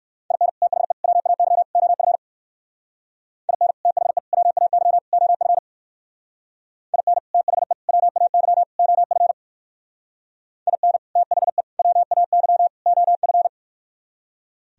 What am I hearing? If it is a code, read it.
Morse code, heard as a keyed tone, about 58 wpm, 700 Hz. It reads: IN THE WAY OF IN THE WAY OF IN THE WAY OF IN THE WAY OF